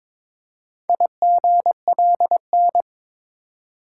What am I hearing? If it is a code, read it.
Morse code, heard as a keyed tone, about 22 wpm, 700 Hz. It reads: IGLN
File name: practice / 22wpm / Four Letter-Number Combinations - 22wpm